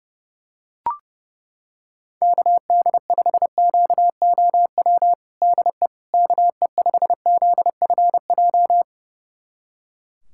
Morse code, heard as a keyed tone, about 30 wpm, 700 Hz. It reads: KD5QOW DE KE5ZFJ
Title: USA Callsign DE Callsign 30wpm